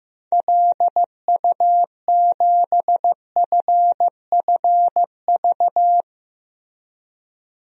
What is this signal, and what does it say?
Morse code, heard as a keyed tone, about 15 wpm, 700 Hz. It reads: LU7FFV
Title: DXCC-100 Call Signs - 6 Characters - 15wpm